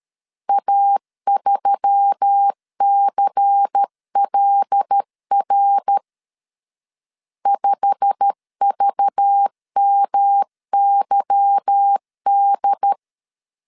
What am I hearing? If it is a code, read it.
Morse code, heard as a keyed tone, about 13 wpm, 800 Hz. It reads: A3CLR 5VMYD
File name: LEZIONI / 8 YJ45 / 8 YJ45 12wpm 2151